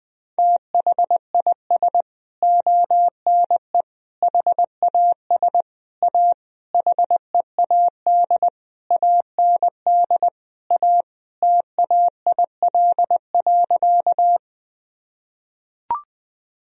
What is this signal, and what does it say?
Morse code, heard as a keyed tone, about 20 wpm, 700 Hz. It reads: THIS ONE HAS A HEAD AND A TAIL.
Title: Sentences from Top 1000 Words - Review Part 3 - 20wpm